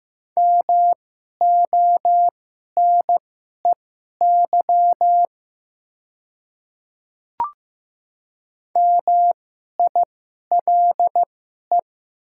Morse code, heard as a keyed tone, about 15 wpm, spaced as for 11 wpm, 700 Hz. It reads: MONEY MILE
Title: 2x Character Spacing - Top 500 Words - 15wpm